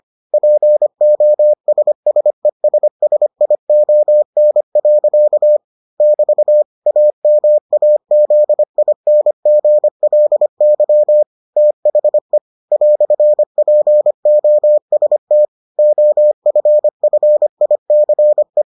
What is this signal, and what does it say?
Morse code, heard as a keyed tone, about 25 wpm, 600 Hz. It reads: POSSESSION. <BT> AMAZINGLY THE "POST OFFICE